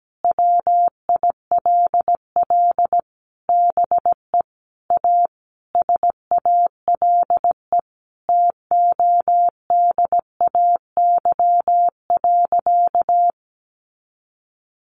Morse code, heard as a keyed tone, about 17 wpm, 700 Hz. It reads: WILL BE A SALE TODAY.
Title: Sentences from Top 1400 Words 17wpm